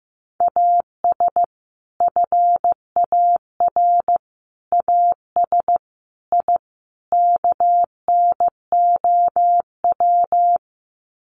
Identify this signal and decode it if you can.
Morse code, heard as a keyed tone, about 15 wpm, 700 Hz. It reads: AS FAR AS I KNOW
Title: Sets of 5 Words 15wpm